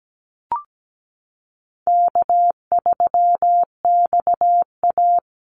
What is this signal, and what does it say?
Morse code, heard as a keyed tone, about 17 wpm, 700 Hz. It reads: K3XA
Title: Call Signs Repeated 2x - 17wpm